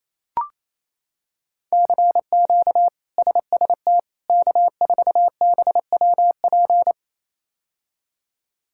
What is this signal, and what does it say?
Morse code, heard as a keyed tone, about 27 wpm, 700 Hz. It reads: CQ SST K4BWP
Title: K1USN SST QSO Practice - 28wpm